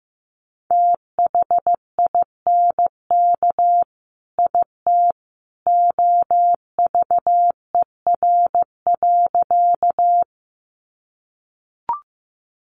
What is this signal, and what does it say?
Morse code, heard as a keyed tone, about 15 wpm, 700 Hz. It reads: THINK IT OVER.
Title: Sentences from Top 200 Words 15wpm